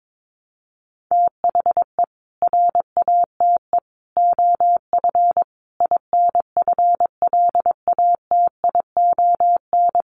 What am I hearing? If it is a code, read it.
Morse code, heard as a keyed tone, about 22 wpm, 700 Hz. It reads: THE RATE OF INFLATION
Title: Sentences from Top 1100 Words 22wpm